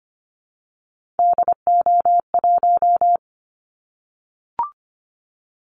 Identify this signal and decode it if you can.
Morse code, heard as a keyed tone, about 25 wpm, 700 Hz. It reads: DO1
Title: Callsign Prefixes for Top 25 DXCC - Repeated 3x - 25wpm